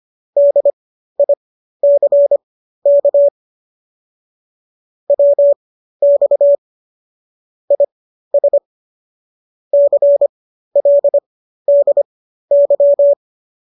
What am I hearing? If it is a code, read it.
Morse code, heard as a keyed tone, about 25 wpm, spaced as for 13 wpm, 570 Hz. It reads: DICK WX IS CLDY